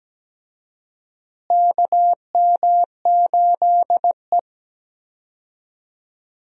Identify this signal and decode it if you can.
Morse code, heard as a keyed tone, about 17 wpm, 700 Hz. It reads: KM8E
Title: USA Call Signs - 4 Characters - Repeated 2x - 17wpm